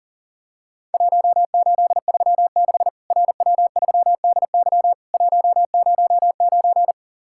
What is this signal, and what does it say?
Morse code, heard as a keyed tone, about 40 wpm, 700 Hz. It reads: 1836 RW3DY 109
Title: Worked All Europe DX Contest - QTC Exchange - 40wpm